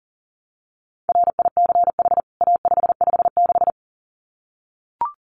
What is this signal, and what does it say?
Morse code, heard as a keyed tone, about 40 wpm, 700 Hz. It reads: RICH A556
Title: CWT Exchange 40wpm